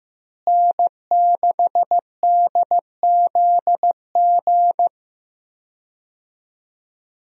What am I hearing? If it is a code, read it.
Morse code, heard as a keyed tone, about 15 wpm, 700 Hz. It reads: N6DZG
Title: USA Call Signs - 5 Characters - Repeated 3x - 15wpm